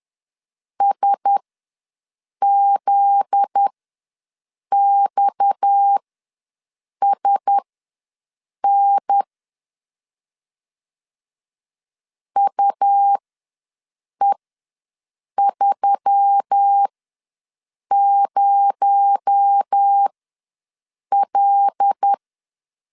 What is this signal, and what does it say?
Morse code, heard as a keyed tone, about 10 wpm, spaced as for 6 wpm, 800 Hz. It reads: SZXSN UE30L